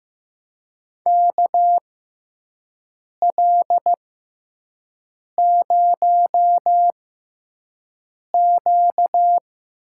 Morse code, heard as a keyed tone, about 15 wpm, spaced as for 5 wpm, 700 Hz. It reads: KL0Q
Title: USA Call Signs - 5 Characters - 6x Character Spacing - 15wpm